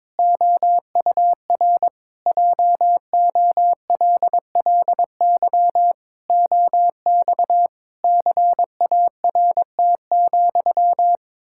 Morse code, heard as a keyed tone, about 21 wpm, 700 Hz. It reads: OUR JOLLY OX CART,
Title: Pangram Challenge v2 - 22wpm